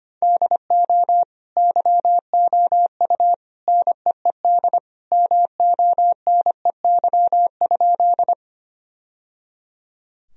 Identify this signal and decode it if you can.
Morse code, heard as a keyed tone, about 25 wpm, 700 Hz. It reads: DO YOU NEED MONEY?
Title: Sentences from Top 1000 Words - Review Part 1 - 25wpm